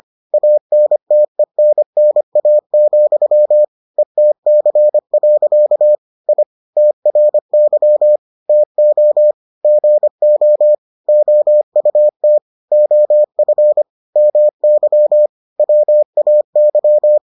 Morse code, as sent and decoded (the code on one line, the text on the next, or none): .- -. - . -. -. .- --..-- / . - -.-. .-.-.- / .. / - .-. -.-- / - --- / --. --- / --- ..- - / --- ..-. / -- -.-- / .-- .- -.--
ANTENNA, ETC. I TRY TO GO OUT OF MY WAY